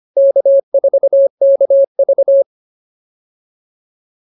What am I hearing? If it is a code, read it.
Morse code, heard as a keyed tone, about 25 wpm, 550 Hz. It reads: K4KV